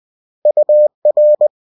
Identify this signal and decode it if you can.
Morse code, heard as a keyed tone, about 20 wpm, 600 Hz. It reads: UR